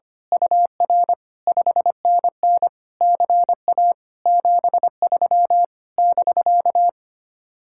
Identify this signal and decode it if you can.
Morse code, heard as a keyed tone, about 24 wpm, 700 Hz. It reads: UR 5NN CA 73 <BK>
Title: Speed-Racing Common QSO Phrases Encore - 25wpm